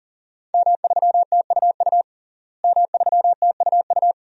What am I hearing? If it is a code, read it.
Morse code, heard as a keyed tone, about 40 wpm, 700 Hz. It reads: M3TUU M3TUU
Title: DXCC-100 Call Signs - 5 Characters - Repeated 2x - 40wpm